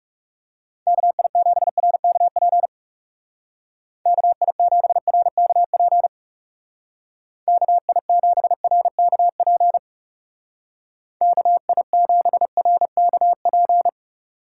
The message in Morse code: KI7RKP KI7RKP KI7RKP KI7RKP

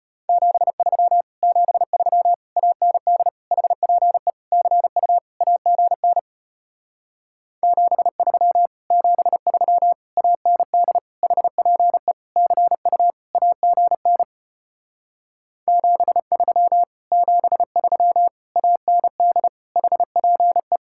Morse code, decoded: 73 73 AND HPE CU AGN 73 73 AND HPE CU AGN 73 73 AND HPE

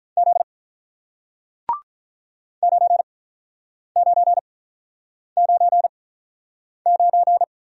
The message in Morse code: Z 9999

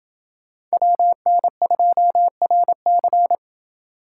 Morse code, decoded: WN2RC

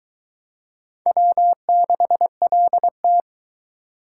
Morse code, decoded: W6LT

